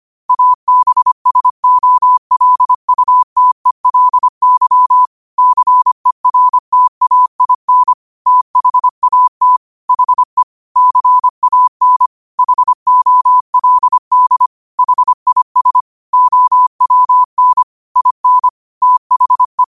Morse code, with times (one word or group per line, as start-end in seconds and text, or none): 0.3-5.1 ABSOLUTELY
5.4-7.9 CERTAIN
8.3-9.6 THAT
9.9-10.4 HE
10.8-12.1 CAN
12.4-14.5 HOLD
14.8-15.8 HIS
16.1-17.6 OWN
18.0-18.5 IN
18.8-19.6 THE